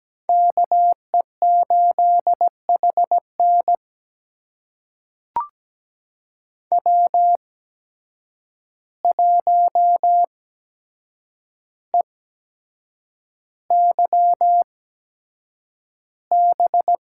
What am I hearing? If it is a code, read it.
Morse code, heard as a keyed tone, about 17 wpm, spaced as for 7 wpm, 700 Hz. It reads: KE8HN W1EYB
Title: USA Call Signs - 5 Characters - 8x Character Spacing - 17wpm